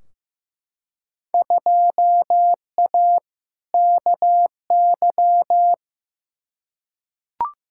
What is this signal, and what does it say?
Morse code, heard as a keyed tone, about 15 wpm, 700 Hz. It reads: 2A KY
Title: Field Day Exchanges - 15wpm